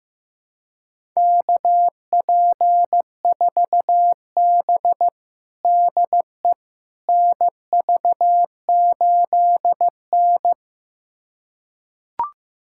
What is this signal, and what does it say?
Morse code, heard as a keyed tone, about 15 wpm, 700 Hz. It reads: KP4B DE NV8N
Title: USA Callsign DE Callsign - 4 Characters - 15wpm